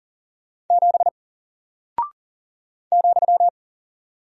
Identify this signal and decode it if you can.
Morse code, heard as a keyed tone, about 40 wpm, 700 Hz. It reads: 7 ,
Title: Single Character (Letter:Number:Punctuation:Prosign) Alt 40wpm